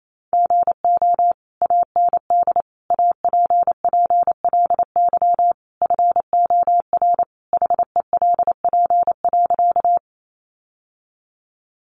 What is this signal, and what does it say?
Morse code, heard as a keyed tone, about 28 wpm, 700 Hz. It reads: GO AND APPLY FOR HELP.
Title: Sentences from Top 1200 Words 28wpm